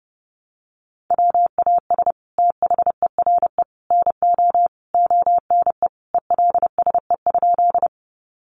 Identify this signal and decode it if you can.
Morse code, heard as a keyed tone, about 29 wpm, 700 Hz. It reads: WAS THERE NO ONE ELSE?
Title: Sentences from Top 800 Words 30wpm